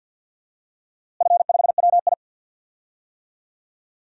Morse code, heard as a keyed tone, about 47 wpm, 700 Hz. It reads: F5WI
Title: DXCC-100 Call Signs - 4 Characters - Repeated 3x - 50wpm